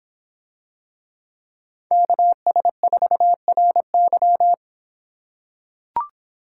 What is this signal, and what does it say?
Morse code, heard as a keyed tone, about 26 wpm, 700 Hz. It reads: KS4RY